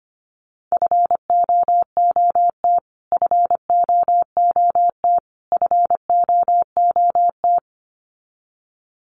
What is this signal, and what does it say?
Morse code, heard as a keyed tone, about 25 wpm, 700 Hz. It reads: FOOT FOOT FOOT
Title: Top 500 Words Repeated 25wpm